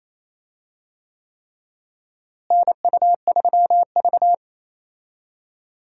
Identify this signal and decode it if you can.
Morse code, heard as a keyed tone, about 28 wpm, 700 Hz. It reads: NU3V
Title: USA Call Signs - 4 Characters - Repeated 3x - 28wpm